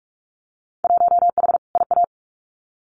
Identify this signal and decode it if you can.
Morse code, heard as a keyed tone, about 46 wpm, 700 Hz. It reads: 1H IA
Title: Winter Field Day Exchanges 45wpm